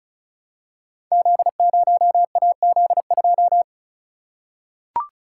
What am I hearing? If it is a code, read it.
Morse code, heard as a keyed tone, about 36 wpm, 700 Hz. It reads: Z0AZ2
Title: Code Groups - 5 Letter:Numbers - 35wpm